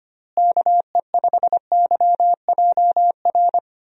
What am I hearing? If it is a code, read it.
Morse code, heard as a keyed tone, about 25 wpm, 700 Hz. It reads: KE5YJR
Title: USA Call Signs - 6 Characters - Repeated 3x - 25wpm